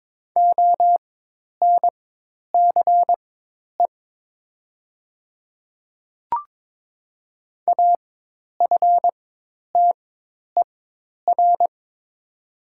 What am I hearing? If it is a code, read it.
Morse code, heard as a keyed tone, about 22 wpm, spaced as for 10 wpm, 700 Hz. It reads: ONCE AFTER
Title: 4x Character Spacing - Top 500 Words - 22wpm